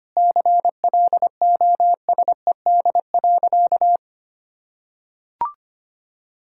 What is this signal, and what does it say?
Morse code, heard as a keyed tone, about 25 wpm, 700 Hz. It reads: CLOSED.
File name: practice / 25wpm / Sentences with 3-Word Phrases - 25wpm